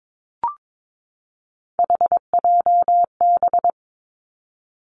Morse code, heard as a keyed tone, about 22 wpm, 700 Hz. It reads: HJB
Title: Three Challenging Letter-Number Combinations 22wpm